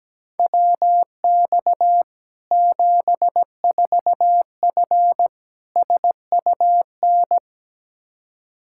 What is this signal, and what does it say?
Morse code, heard as a keyed tone, about 17 wpm, 700 Hz. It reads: WX 74F SUN
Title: Practice QSO - Exchange 2 - Info - 17wpm